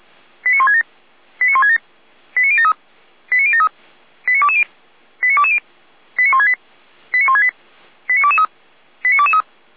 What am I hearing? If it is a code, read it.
Five-tone selective call (ZVEI), heard as quick runs of five tones, tones of about 80 ms. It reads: 89167 89167 89073 89073 89220 89220 89168 89168 89223 89223
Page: /455/SDR/ZVEI/89001/